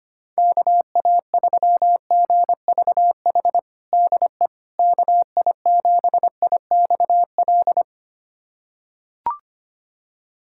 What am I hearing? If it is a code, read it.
Morse code, heard as a keyed tone, about 25 wpm, 700 Hz. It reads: KA3GVH DE KI7IXL
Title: USA Callsign DE Callsign - 6 Characters - 25wpm